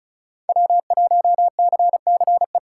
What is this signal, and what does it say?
Morse code, heard as a keyed tone, about 36 wpm, 700 Hz. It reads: W1CCE
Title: K1USN SST QSO Practice - 35wpm